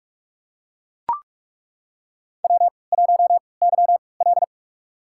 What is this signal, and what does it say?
Morse code, heard as a keyed tone, about 48 wpm, spaced as for 26 wpm, 690 Hz. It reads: W1YL
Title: USA Call Signs - 4 Characters - 3x Character Spacing - 45wpm